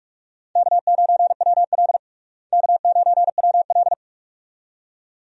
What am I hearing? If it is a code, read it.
Morse code, heard as a keyed tone, about 45 wpm, 700 Hz. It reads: K9WL K9WL
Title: USA Call Signs - 4 Characters - Repeated 2x - 45wpm